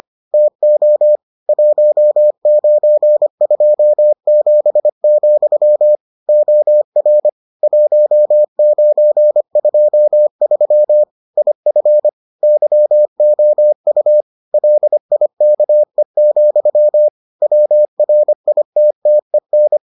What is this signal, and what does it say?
Morse code, heard as a keyed tone, about 25 wpm, 600 Hz. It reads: TO 1927, OR 1923 IF YOU LIKE, WRITTEN